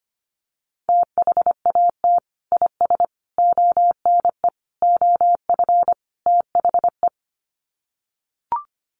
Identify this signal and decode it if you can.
Morse code, heard as a keyed tone, about 25 wpm, 700 Hz. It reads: THAT IS ONE OF THE
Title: Sets of 5 Words Encore 25wpm